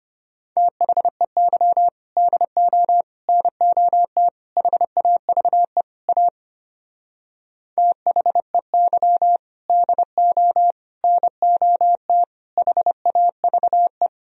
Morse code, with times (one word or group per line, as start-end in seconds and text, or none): 0.6-1.9 THEY
2.2-3.0 DO
3.3-4.3 NOT
4.6-5.8 HAVE
6.1-6.3 A
7.8-9.4 THEY
9.7-10.7 DO
11.0-12.2 NOT
12.6-14.1 HAVE